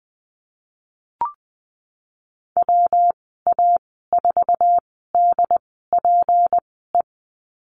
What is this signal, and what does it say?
Morse code, heard as a keyed tone, about 20 wpm, spaced as for 14 wpm, 700 Hz. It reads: WA4DPE
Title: 173 Lesson 28g - Number 9 - Call Signs 20wpm